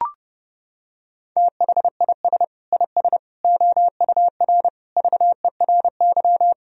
Morse code, heard as a keyed tone, about 30 wpm, 700 Hz. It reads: THIS IS OUR VERY
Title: Sentences from Top 1000 Words - Review Part 4 - 30wpm